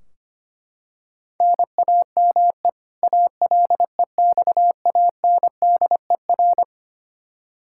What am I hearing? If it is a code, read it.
Morse code, heard as a keyed tone, about 25 wpm, 700 Hz. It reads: NAME ALEXANDER
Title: Practice QSO - Exchange 1 - Beginning - 25wpm